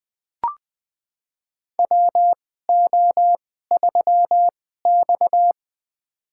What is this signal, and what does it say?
Morse code, heard as a keyed tone, about 20 wpm, spaced as for 14 wpm, 700 Hz. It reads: WO3X
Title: USA Call Signs - 4 Characters - 2x Character Spacing - 20wpm